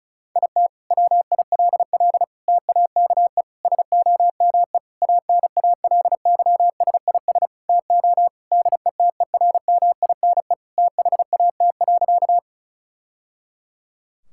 Morse code, heard as a keyed tone, about 36 wpm, 700 Hz. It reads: IT WILL TAKE SOME ANALYSIS TO DETERMINE THAT.
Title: Sentences from Top 1000 Words - Review Part 5 - 35wpm